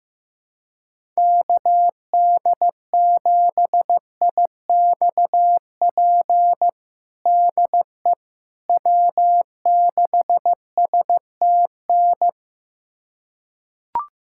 Morse code, heard as a keyed tone, about 16 wpm, 700 Hz. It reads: KD7IXP DE W6STN
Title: USA Callsign DE Callsign 15wpm